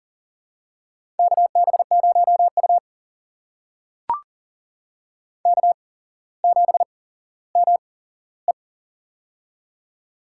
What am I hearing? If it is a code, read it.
Morse code, heard as a keyed tone, about 40 wpm, spaced as for 17 wpm, 700 Hz. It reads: KB0U K7ME
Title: USA Call Signs - 4 Characters - 8x Character Spacing - 40wpm